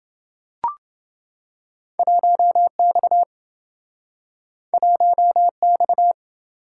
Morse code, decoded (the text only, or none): 1X 1X